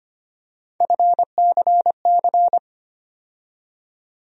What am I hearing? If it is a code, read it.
Morse code, heard as a keyed tone, about 25 wpm, 700 Hz. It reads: FCC
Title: USA Call Sign Postfixes - 3 Letters 25wpm